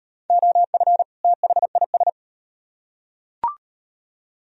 Morse code, heard as a keyed tone, about 38 wpm, 700 Hz. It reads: OF THIS